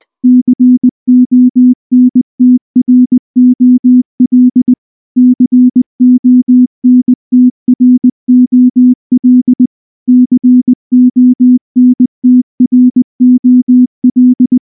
Morse code, heard as a keyed tone, about 19 wpm, 250 Hz. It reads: CONTROL CONTROL CONTROL
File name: morse / morse-training-20WPM-4